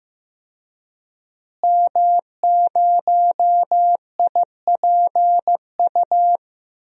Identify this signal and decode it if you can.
Morse code, heard as a keyed tone, about 16 wpm, 700 Hz. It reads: M0IPU
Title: DXCC-100 Call Signs - 5 Characters - Repeated 2x - 15wpm